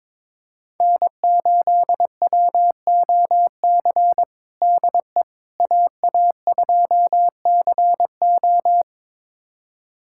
Morse code, heard as a keyed tone, about 22 wpm, 700 Hz. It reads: N8WOC DE AA2CO
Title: USA Callsign DE Callsign - 5 Characters - 22wpm